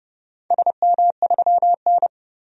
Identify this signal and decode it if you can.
Morse code, heard as a keyed tone, about 31 wpm, 700 Hz. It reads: SM3N
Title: DXCC-100 Call Signs - 4 Characters - 30wpm